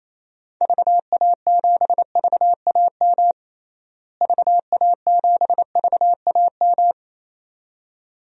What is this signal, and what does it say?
Morse code, heard as a keyed tone, about 28 wpm, 700 Hz. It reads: VA7VAM VA7VAM
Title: DXCC-100 Call Signs - 6 Characters - Repeated 2x - 28wpm